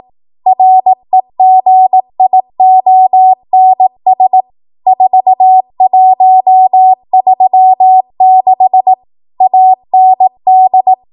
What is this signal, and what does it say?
Morse code, heard as a keyed tone, about 18 wpm, 750 Hz. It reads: REGIONS 4136 AND
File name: Morse / 18WPM